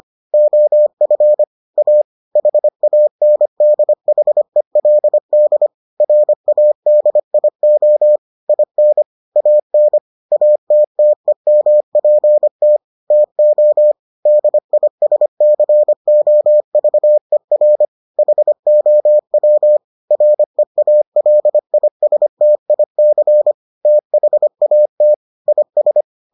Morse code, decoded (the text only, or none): OF A HANDHELD RADIO IN AN ATTEMPT TO DISCOVER HOW REALISTIC THAT IS